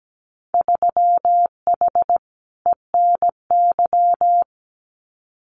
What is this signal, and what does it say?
Morse code, heard as a keyed tone, about 17 wpm, 700 Hz. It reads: 3H ENY
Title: Winter Field Day Exchanges Encore 17wpm